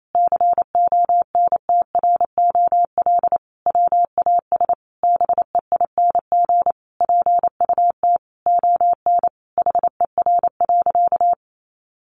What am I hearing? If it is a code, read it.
Morse code, heard as a keyed tone, about 28 wpm, 700 Hz. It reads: CONTROL WAS BEING PUT ON HER.
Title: Sentences from Top 1300 Words 28wpm